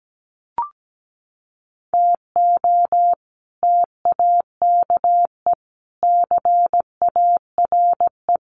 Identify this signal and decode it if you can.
Morse code, heard as a keyed tone, about 17 wpm, 700 Hz. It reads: TO TAKE CARE
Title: Sets of 4 Words 17wpm